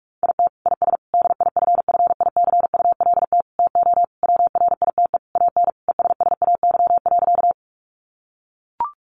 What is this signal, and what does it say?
Morse code, heard as a keyed tone, about 44 wpm, 700 Hz. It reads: IT IS DIFFICULT TO WRITE AN ESSAY.
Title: Sentences from Top 1500 Words 45wpm